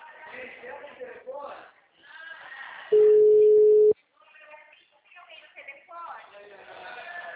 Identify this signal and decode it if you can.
Ringback tone (Europe, 425 Hz), with speech.
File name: out- -2003- - - .9533